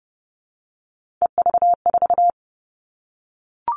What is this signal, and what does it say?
Morse code, heard as a keyed tone, about 30 wpm, 700 Hz. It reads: EV4
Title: All Cut Numbers - 3-Digit 30wpm